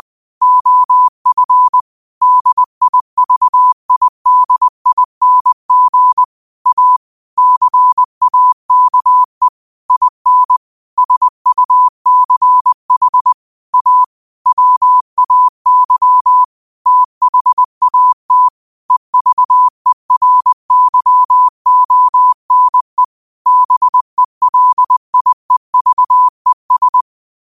Morse code, heard as a keyed tone, about 20 wpm, 1000 Hz. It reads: OF DIVIDING A CAKE IN SUCH A WAY THAT EVERYONE BELIEVES
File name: qotdfiles / 20wpm